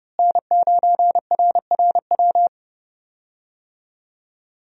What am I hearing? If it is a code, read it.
Morse code, heard as a keyed tone, about 30 wpm, 700 Hz. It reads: N9RRW
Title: USA Call Signs - 5 Characters - Repeated 2x - 30wpm